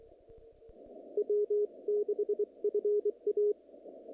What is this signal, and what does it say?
Morse code, heard as a keyed tone, about 26 wpm, spaced as for 22 wpm, 420 Hz. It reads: W6FA
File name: contests / naqp-cw-k5tr-op-k5pi- -2200